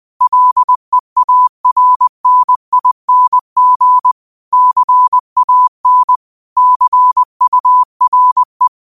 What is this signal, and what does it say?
Morse code, heard as a keyed tone, about 20 wpm, 1000 Hz. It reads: LEARNING CAN CURE
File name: qotdfiles / 20wpm